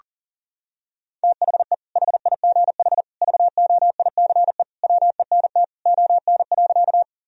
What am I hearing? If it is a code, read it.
Morse code, heard as a keyed tone, about 40 wpm, 700 Hz. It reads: THE HIGH VOICE WENT ON.
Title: Sentences from Top 1000 Words - Review Part 1 - 40wpm